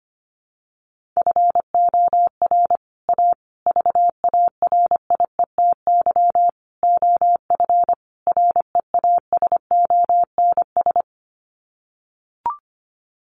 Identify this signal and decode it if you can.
Morse code, heard as a keyed tone, about 25 wpm, 700 Hz. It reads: FOR A VARIETY OF REASONS
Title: Sets of 5 Words Encore 25wpm